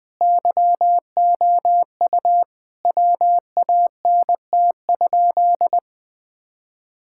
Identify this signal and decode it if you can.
Morse code, heard as a keyed tone, about 20 wpm, 700 Hz. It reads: YOU WANT?
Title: Sentences with 2-Word Phrases - Spoken Beginning - 20wpm